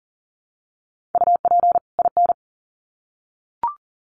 Morse code, heard as a keyed tone, about 40 wpm, 700 Hz. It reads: UP IN
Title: Sets of 2 Words 40wpm